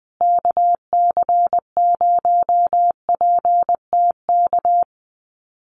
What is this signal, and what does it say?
Morse code, heard as a keyed tone, about 20 wpm, 700 Hz. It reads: KC0PTK
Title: USA Call Signs Repeated 2x 20wpm